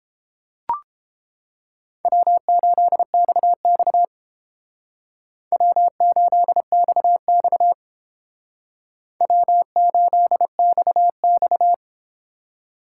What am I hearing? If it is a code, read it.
Morse code, heard as a keyed tone, about 30 wpm, 700 Hz. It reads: W8XX W8XX W8XX